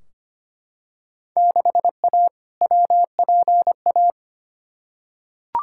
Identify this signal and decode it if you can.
Morse code, heard as a keyed tone, about 25 wpm, 700 Hz. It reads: 6A WPA
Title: Field Day Exchanges Encore - 25wpm